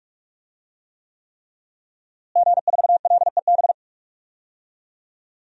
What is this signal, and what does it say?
Morse code, heard as a keyed tone, about 47 wpm, 700 Hz. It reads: G4LEB
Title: DXCC-100 Call Signs - 5 Characters - Repeated 3x - 45wpm